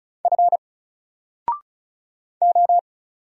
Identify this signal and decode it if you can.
Morse code, heard as a keyed tone, about 35 wpm, 700 Hz. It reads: F O
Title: Single Character (Letter:Number:Punctuation:Prosign) 35wpm